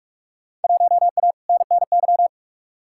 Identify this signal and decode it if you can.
Morse code, heard as a keyed tone, about 47 wpm, 700 Hz. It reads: 1A NNY
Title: Field Day Exchanges - 45wpm